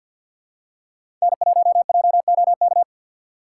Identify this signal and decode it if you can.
Morse code, heard as a keyed tone, about 52 wpm, 700 Hz. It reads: N1JOK
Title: USA Call Signs - 5 Characters - 4x Character Spacing - 50wpm